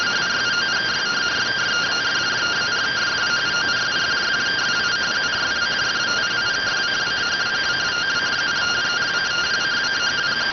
Two-tone teleprinter signal, about 170 Hz shift.